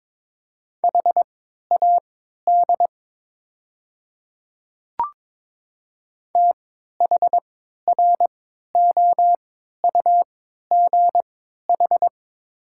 Morse code, heard as a keyed tone, about 22 wpm, spaced as for 13 wpm, 690 Hz. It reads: HAD THROUGH
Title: 3x Character Spacing - Top 200 Words - 22wpm